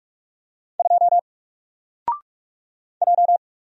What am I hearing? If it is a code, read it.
Morse code, heard as a keyed tone, about 46 wpm, 700 Hz. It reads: 2 J
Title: Single Letter-Number Alt 45wpm